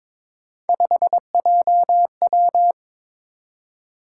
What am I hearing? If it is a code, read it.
Morse code, heard as a keyed tone, about 21 wpm, 700 Hz. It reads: 5JW
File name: practice / 22wpm / Three Letter-Number Combinations - 22wpm